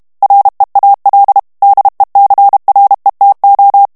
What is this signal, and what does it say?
Morse code, heard as a keyed tone, about 32 wpm, 800 Hz. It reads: REAL DECRETO
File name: ESPAGNOL TEXTE EN CLAIR 2 VITESSE 30 MOTS MINUTES